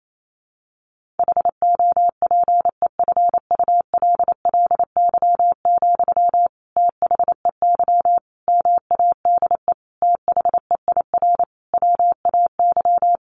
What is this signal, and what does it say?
Morse code, heard as a keyed tone, about 28 wpm, 700 Hz. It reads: HOPEFULLY, THEY MADE THEIR WAY
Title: Sentences from Top 1500 Words 28wpm